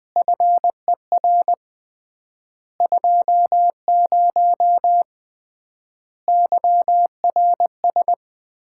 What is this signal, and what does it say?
Morse code, heard as a keyed tone, about 21 wpm, 700 Hz. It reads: FER 20 YRS
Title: Common QSO Phrases - 3x Word-Spacing - 20wpm